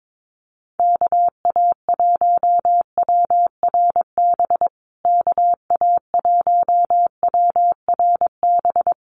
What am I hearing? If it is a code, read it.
Morse code, heard as a keyed tone, about 22 wpm, 700 Hz. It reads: KA1WRB KA1WRB
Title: USA Call Signs Repeated 3x 22wpm